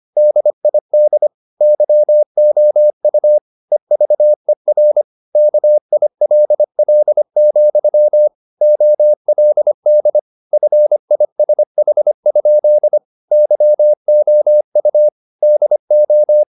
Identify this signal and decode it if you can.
Morse code, heard as a keyed tone, about 25 wpm, 600 Hz. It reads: DID YOU EVER KILL, OLD FISH? YOU DO